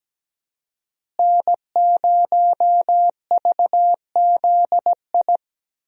Morse code, heard as a keyed tone, about 17 wpm, 700 Hz. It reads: N0VZI
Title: USA Call Signs - 5 Characters - Repeated 2x - 17wpm